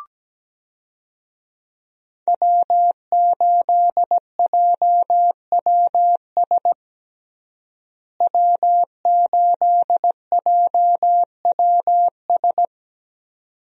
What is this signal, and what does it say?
Morse code, heard as a keyed tone, about 17 wpm, 700 Hz. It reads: W8JWS W8JWS
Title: USA Call Signs - 5 Characters - Repeated 2x - 17wpm